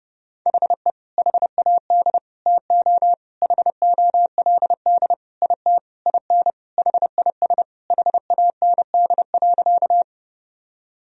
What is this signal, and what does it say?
Morse code, heard as a keyed tone, about 30 wpm, 700 Hz. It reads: HE HAD TO HOLD IT IN HIS HAND.
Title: Sentences from Top 1000 Words - Review Part 2 - 30wpm